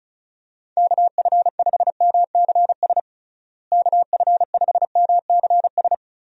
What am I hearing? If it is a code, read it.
Morse code, heard as a keyed tone, about 34 wpm, 700 Hz. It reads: KF5MCS KF5MCS
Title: USA Call Signs - 6 Characters - Repeated 2x - 35wpm